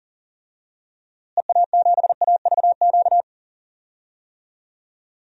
Morse code, heard as a keyed tone, about 40 wpm, 700 Hz. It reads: EA7AVQ
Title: DXCC-100 Call Signs - 6 Characters - Repeated 3x - 40wpm